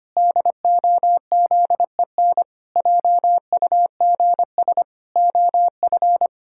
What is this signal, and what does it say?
Morse code, heard as a keyed tone, about 25 wpm, 700 Hz. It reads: DOZEN JUGS OF
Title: Pangram Challenge v2 - 25wpm